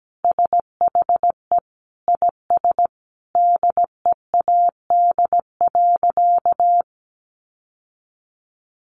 Morse code, heard as a keyed tone, about 17 wpm, 700 Hz. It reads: SHE IS DEAD.
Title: Sentences from Top 1500 Words 17wpm